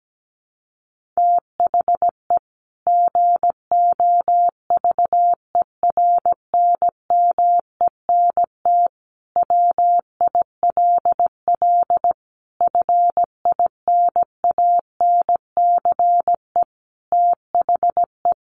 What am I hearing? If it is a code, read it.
Morse code, heard as a keyed tone, about 17 wpm, 700 Hz. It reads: THE GOVERNMENT WILL FINANCE THE
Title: Sentences from Top 1500 Words 17wpm